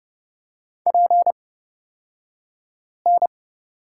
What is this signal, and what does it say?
Morse code, heard as a keyed tone, about 30 wpm, spaced as for 5 wpm, 700 Hz. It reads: PN
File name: Lesson 38d - Letter q - Two Characters 30wpm